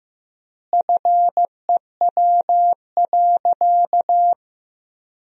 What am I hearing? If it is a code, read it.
Morse code, heard as a keyed tone, about 15 wpm, 700 Hz. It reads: FEW.
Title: 3x Word-Spacing - Sentences from Top 300 Words - 15wpm